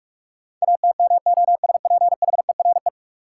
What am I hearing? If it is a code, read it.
Morse code, heard as a keyed tone, about 44 wpm, 700 Hz. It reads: ATMOSPHERE
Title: Sentences from Top 1600 Words - New Word Drill - 45wpm